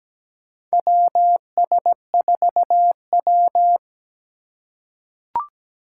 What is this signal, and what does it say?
Morse code, heard as a keyed tone, about 17 wpm, 700 Hz. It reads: WS4W
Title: USA Call Signs - 4 Characters - 4x Character Spacing - 17wpm